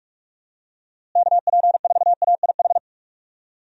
Morse code, heard as a keyed tone, about 43 wpm, 700 Hz. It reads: KP4AIH
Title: DXCC-100 Call Signs - 6 Characters - Repeated 2x - 45wpm